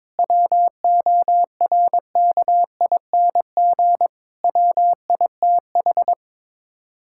Morse code, heard as a keyed tone, about 22 wpm, 700 Hz. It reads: WORKING WITH
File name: practice / 22wpm / Sets of 6 Words - 22wpm